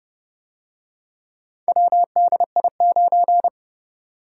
Morse code, heard as a keyed tone, about 30 wpm, 700 Hz. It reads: WDI9